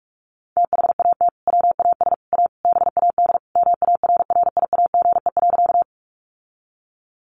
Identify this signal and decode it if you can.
Morse code, heard as a keyed tone, about 44 wpm, 700 Hz. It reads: THAT WAS A BAD MARRIAGE.
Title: Sentences from Top 800 Words 45wpm